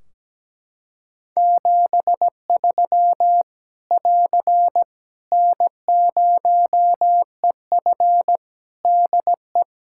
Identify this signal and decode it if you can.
Morse code, heard as a keyed tone, about 17 wpm, 700 Hz. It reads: 73 <AR> N0EF DE
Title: Practice QSO - Exchange 4 - Ending - 17wpm